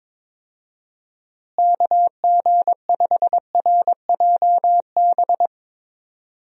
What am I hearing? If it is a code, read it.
Morse code, heard as a keyed tone, about 22 wpm, 700 Hz. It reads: KG5RJB
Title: USA Call Signs - 6 Characters - Repeated 3x - 22wpm